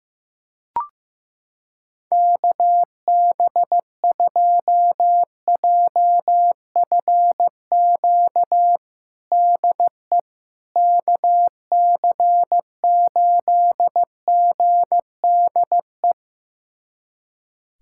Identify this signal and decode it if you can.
Morse code, heard as a keyed tone, about 15 wpm, 700 Hz. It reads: KB2JFQ DE KC8GDE
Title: USA Callsign DE Callsign - 6 Characters - 15wpm